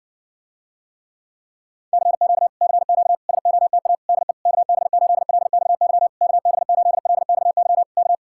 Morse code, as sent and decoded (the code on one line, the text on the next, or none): -.-. --.- / -.-. --.- / ... --- - .- / -.. / -.- -... ---.. .-.. -..- -.-- / -.- -... ---.. .-.. -..- -.-- / -.-
CQ CQ SOTA DE KB8LXY KB8LXY K